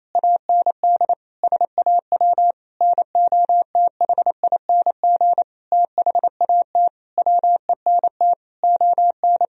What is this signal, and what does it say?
Morse code, heard as a keyed tone, about 28 wpm, 700 Hz. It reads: AND SAW NOTHING THAT WENT ON
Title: Sentences from Top 1000 Words - Review Part 3 - 28wpm